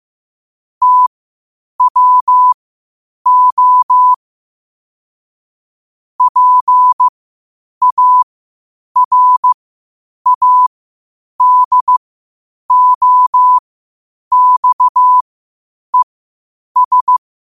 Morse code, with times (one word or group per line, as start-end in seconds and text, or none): 0.8-4.1 TWO
6.2-17.2 PARADOXES